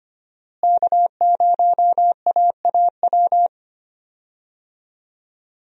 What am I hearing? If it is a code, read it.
Morse code, heard as a keyed tone, about 25 wpm, 700 Hz. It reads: K0AAW
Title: USA Call Signs - 5 Characters - Repeated 2x - 25wpm